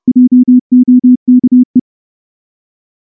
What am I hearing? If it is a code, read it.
Morse code, heard as a keyed tone, about 29 wpm, 250 Hz. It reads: JOKE
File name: morse-training-30WPM